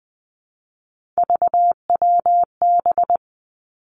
Morse code, heard as a keyed tone, about 20 wpm, 700 Hz. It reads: VWB
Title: Three Challenging Letter-Number Combinations 20wpm